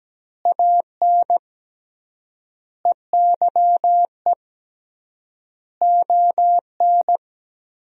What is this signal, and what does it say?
Morse code, heard as a keyed tone, about 17 wpm, 700 Hz. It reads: AN EYE ON
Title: 3x Word-Spacing - Sentences from Top 300 Words - 17wpm